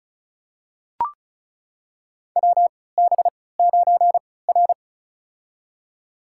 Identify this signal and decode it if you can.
Morse code, heard as a keyed tone, about 36 wpm, spaced as for 20 wpm, 690 Hz. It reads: WB9R
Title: USA Call Signs - 3x Character Spacing - 35wpm